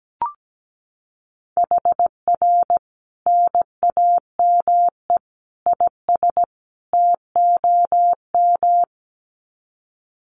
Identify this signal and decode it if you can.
Morse code, heard as a keyed tone, about 17 wpm, 700 Hz. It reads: HR NAME IS TOM
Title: Common QSO Phrases 17wpm